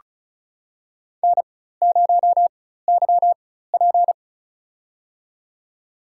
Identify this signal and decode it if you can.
Morse code, heard as a keyed tone, about 36 wpm, spaced as for 16 wpm, 700 Hz. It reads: N0YP
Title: USA Call Signs - 4 Characters - 4x Character Spacing - 35wpm